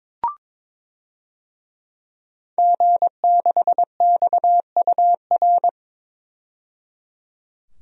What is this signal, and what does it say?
Morse code, heard as a keyed tone, about 22 wpm, 700 Hz. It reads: G6XUR
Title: DXCC-100 Call Signs - 5 Characters - 22wpm